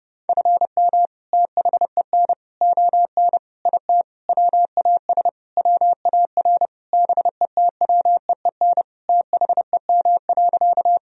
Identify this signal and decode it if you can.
Morse code, heard as a keyed tone, about 32 wpm, 700 Hz. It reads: FM THEN ON IT WAS WAR BETWEEN THEM.
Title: Sentences from Top 400 Words and Abbreviations 30wpm